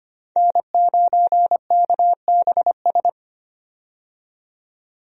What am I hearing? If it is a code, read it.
Morse code, heard as a keyed tone, about 25 wpm, 700 Hz. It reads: N9KBS
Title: USA Call Signs - 5 Characters - 25wpm